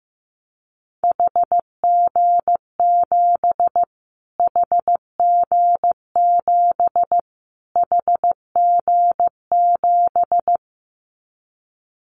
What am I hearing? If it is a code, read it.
Morse code, heard as a keyed tone, about 15 wpm, 700 Hz. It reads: HG7 HG7 HG7
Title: Callsign Prefixes for Top 25 DXCC - Repeated 3x - 15wpm